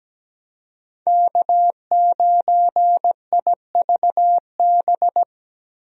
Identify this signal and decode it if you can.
Morse code, heard as a keyed tone, about 17 wpm, 700 Hz. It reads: K9IVB
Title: USA Call Signs - 5 Characters - Repeated 3x - 17wpm